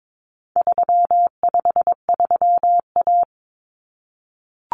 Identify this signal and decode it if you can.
Morse code, heard as a keyed tone, about 22 wpm, 700 Hz. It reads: 353A
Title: Common Cut Numbers - 4-Digit 22wpm